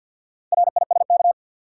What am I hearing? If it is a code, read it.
Morse code, heard as a keyed tone, about 48 wpm, 700 Hz. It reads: RISK
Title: Sentences from Top 600 Words - New Word Drill - 50wpm